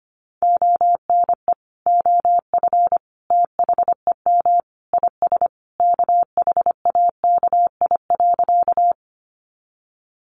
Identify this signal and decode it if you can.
Morse code, heard as a keyed tone, about 25 wpm, 700 Hz. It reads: ONE OF THEM IS KHAKI.
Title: Additional Easy Phrases 25wpm